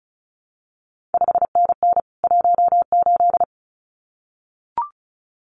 Morse code, heard as a keyed tone, about 36 wpm, 700 Hz. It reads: 5NN 18
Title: CQ WW DX Contest Exchange Encore - 35wpm